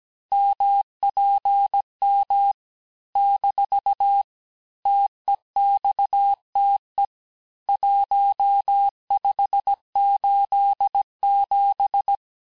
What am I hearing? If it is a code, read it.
Morse code, heard as a keyed tone, about 17 wpm, 800 Hz. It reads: MPM - TEXTE 1587
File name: cours-du-lundi files / cours-1587-texte-chiffre-vitesse-16MPM